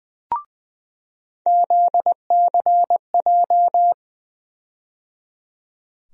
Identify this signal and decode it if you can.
Morse code, heard as a keyed tone, about 21 wpm, 700 Hz. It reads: ZCJ
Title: USA Call Sign Postfixes - 3 Letters 20wpm